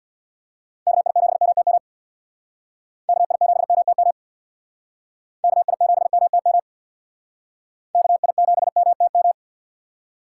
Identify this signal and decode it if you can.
Morse code, heard as a keyed tone, about 66 wpm, 700 Hz. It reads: KI7MTM KI7MTM KI7MTM KI7MTM